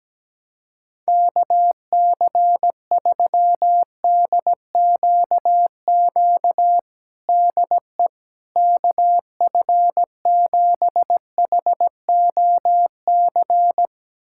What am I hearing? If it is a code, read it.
Morse code, heard as a keyed tone, about 17 wpm, 700 Hz. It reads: KC3DQQ DE KF7HOC
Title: USA Callsign DE Callsign 17wpm